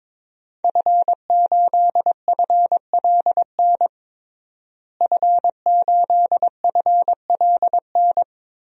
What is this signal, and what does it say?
Morse code, heard as a keyed tone, about 21 wpm, 700 Hz. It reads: F8FLN F8FLN
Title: DXCC-100 Call Signs - 5 Characters - Repeated 2x - 22wpm